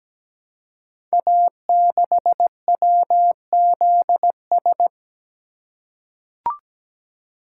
Morse code, heard as a keyed tone, about 17 wpm, 700 Hz. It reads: A6WZS
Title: Code Groups - 5 Letter:Numbers - 17wpm